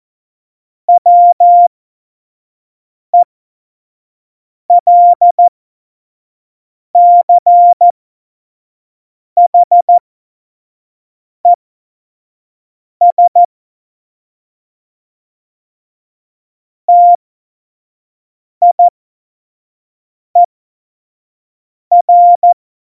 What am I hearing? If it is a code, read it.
Morse code, heard as a keyed tone, about 14 wpm, spaced as for 5 wpm, 700 Hz. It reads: WELCHES TIER
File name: Morsekurs / Witze / Witz 25. Dezember 2018, 25 F70